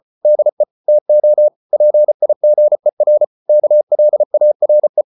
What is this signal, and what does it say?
Morse code, heard as a keyed tone, about 34 wpm, 600 Hz. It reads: DE TO PIGER KLARE
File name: Ferie QRQ 34wpm